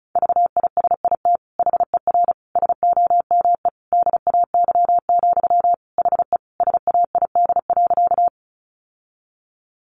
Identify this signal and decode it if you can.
Morse code, heard as a keyed tone, about 34 wpm, 700 Hz. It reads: VISIT HER SOME DAY, HE SAID.
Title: Sentences from Top 900 Words 35wpm